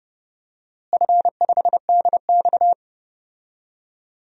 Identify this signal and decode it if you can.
Morse code, heard as a keyed tone, about 30 wpm, 700 Hz. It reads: F5DX
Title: DXCC-100 Call Signs - 4 Characters - Repeated 3x - 30wpm